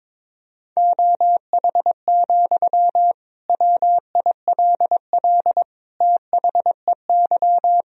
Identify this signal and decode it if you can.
Morse code, heard as a keyed tone, about 21 wpm, 700 Hz. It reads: OH, WILL THEY
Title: Sentences from Top 1000 Words - Review Part 4 - 22wpm